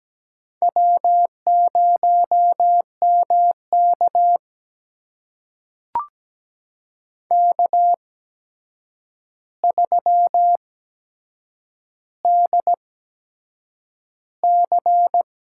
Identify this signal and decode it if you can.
Morse code, heard as a keyed tone, about 17 wpm, spaced as for 7 wpm, 700 Hz. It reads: W0MK K3DC
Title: USA Call Signs - 4 Characters - 8x Character Spacing - 17wpm